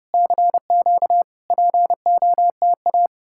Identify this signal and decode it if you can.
Morse code, heard as a keyed tone, about 35 wpm, 700 Hz. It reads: CQ POTA